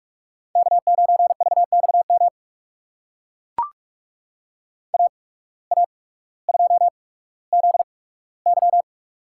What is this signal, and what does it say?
Morse code, heard as a keyed tone, about 44 wpm, spaced as for 19 wpm, 700 Hz. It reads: K9VXM AA2ZY
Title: USA Call Signs - 5 Characters - 8x Character Spacing - 45wpm